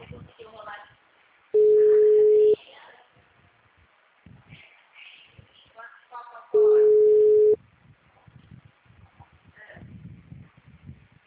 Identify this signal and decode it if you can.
Ringback tone (Europe, 425 Hz), with speech.